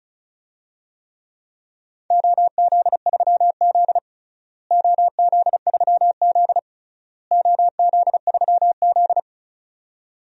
Morse code, heard as a keyed tone, about 34 wpm, 700 Hz. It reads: OZ3Z OZ3Z OZ3Z
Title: DXCC-100 Call Signs - 4 Characters - Repeated 3x - 35wpm